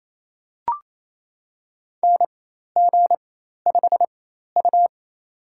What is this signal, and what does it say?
Morse code, heard as a keyed tone, about 28 wpm, spaced as for 13 wpm, 700 Hz. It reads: NG5U
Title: USA Call Signs - 4 Characters - 4x Character Spacing - 28wpm